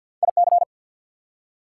Morse code, heard as a keyed tone, about 54 wpm, 690 Hz. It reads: IC
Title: Prefixes and Suffixes 50wpm